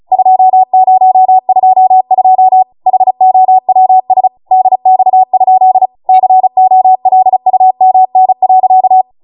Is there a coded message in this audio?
Yes: Morse code from 0.1 to 9.0 s, heard as a keyed tone, about 36 wpm, 750 Hz. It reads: 2022 HOWS DX? COLUMN.